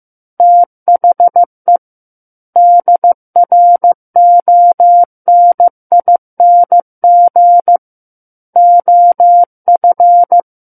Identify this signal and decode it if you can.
Morse code, heard as a keyed tone, about 15 wpm, 700 Hz. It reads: THE DRONING OF
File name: Chapter 6 - 15 WPM